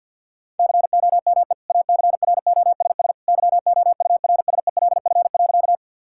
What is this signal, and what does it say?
Morse code, heard as a keyed tone, about 50 wpm, 700 Hz. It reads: COME ACROSS YOURSELF.
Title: Sentences from Top 1000 Words - Review Part 5 - 50wpm